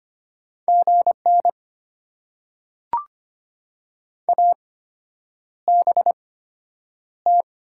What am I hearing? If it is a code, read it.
Morse code, heard as a keyed tone, about 25 wpm, spaced as for 9 wpm, 700 Hz. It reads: GN ABT
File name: Top 90 QSO Elements - 8x Character Spacing - 25wpm